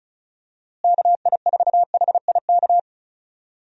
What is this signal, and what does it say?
Morse code, heard as a keyed tone, about 34 wpm, 700 Hz. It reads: KI4HIK
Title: USA Call Signs - 6 Characters - Repeated 3x - 35wpm